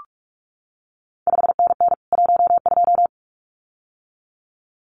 Morse code, heard as a keyed tone, about 46 wpm, 700 Hz. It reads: 5NN 12
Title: CQ WW DX Contest Exchange Encore - 45wpm